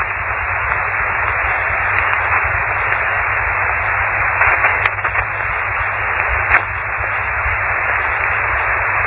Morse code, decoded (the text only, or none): none